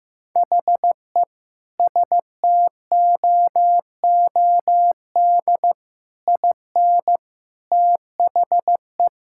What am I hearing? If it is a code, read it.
Morse code, heard as a keyed tone, about 15 wpm, 700 Hz. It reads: HE STOOD IN THE
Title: Sentences from Top 1000 Words - Review Part 6 - 15wpm